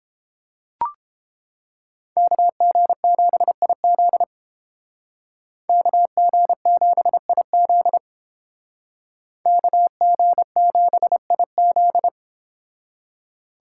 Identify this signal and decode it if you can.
Morse code, heard as a keyed tone, about 30 wpm, 700 Hz. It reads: KG7IZ KG7IZ KG7IZ